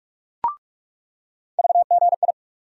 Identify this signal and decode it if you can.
Morse code, heard as a keyed tone, about 46 wpm, 700 Hz. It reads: VGI
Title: USA Call Sign Postfixes - 3 Letters 45wpm